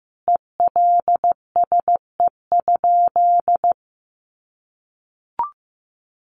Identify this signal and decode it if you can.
Morse code, heard as a keyed tone, about 14 wpm, 700 Hz. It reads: ELSE?
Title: Sentences from Top 800 Words 15wpm